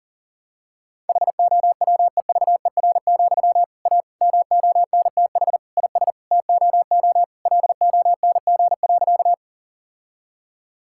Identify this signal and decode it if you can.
Morse code, heard as a keyed tone, about 40 wpm, 700 Hz. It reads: HOWEVER, A MONTH IS TOO LONG.
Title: Sentences from Top 1000 Words - Review Part 5 - 40wpm